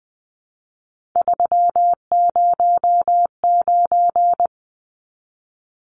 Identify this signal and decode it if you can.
Morse code, heard as a keyed tone, about 20 wpm, 700 Hz. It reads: 309
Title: Numbers - 3-Digit 20wpm